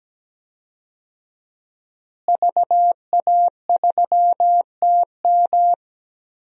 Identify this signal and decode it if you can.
Morse code, heard as a keyed tone, about 17 wpm, 700 Hz. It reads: VA3TM
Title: DXCC-100 Call Signs - 5 Characters - Repeated 2x - 17wpm